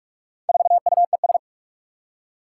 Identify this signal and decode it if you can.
Morse code, heard as a keyed tone, about 44 wpm, 700 Hz. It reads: 4UES
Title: Four Challenging Letter-Number Combinations 45wpm